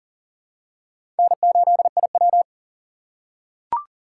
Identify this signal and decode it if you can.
Morse code, heard as a keyed tone, about 40 wpm, 700 Hz. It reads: N8IW
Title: USA Call Signs - 4 Characters - 4x Character Spacing - 40wpm